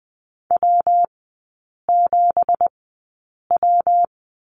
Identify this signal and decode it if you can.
Morse code, heard as a keyed tone, about 20 wpm, spaced as for 8 wpm, 700 Hz. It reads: W7W
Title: 166 Lesson 27e - Number 7 - Three Characters 20wpm